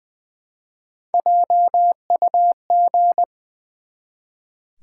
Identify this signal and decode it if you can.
Morse code, heard as a keyed tone, about 20 wpm, 700 Hz. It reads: JUG
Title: Easy Words with Hints - 20wpm